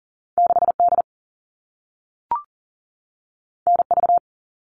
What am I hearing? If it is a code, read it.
Morse code, heard as a keyed tone, about 38 wpm, 700 Hz. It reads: 6D NV
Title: All Cut Numbers - 2-Digit 40wpm